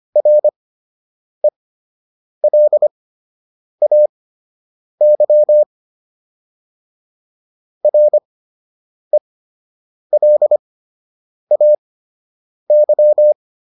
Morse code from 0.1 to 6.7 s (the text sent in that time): RELAY